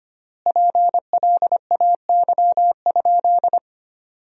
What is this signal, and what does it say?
Morse code, heard as a keyed tone, about 25 wpm, 700 Hz. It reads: PLAY?
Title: 3x Word-Spacing - Sentences from Top 200 Words - 25wpm